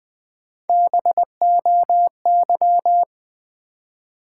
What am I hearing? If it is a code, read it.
Morse code, heard as a keyed tone, about 20 wpm, 700 Hz. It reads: BOY
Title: Three Letter Words - 20wpm